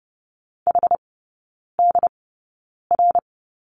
Morse code, heard as a keyed tone, about 30 wpm, spaced as for 9 wpm, 690 Hz. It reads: HDR
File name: Lesson 16e - Letter c - Three Characters 30wpm